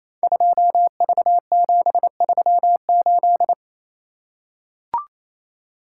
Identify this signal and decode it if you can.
Morse code, heard as a keyed tone, about 28 wpm, 700 Hz. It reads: 2V738
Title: Code Groups - 5 Letter:Numbers - 28wpm